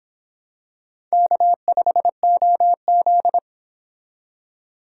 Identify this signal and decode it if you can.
Morse code, heard as a keyed tone, about 26 wpm, 700 Hz. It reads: K5OZ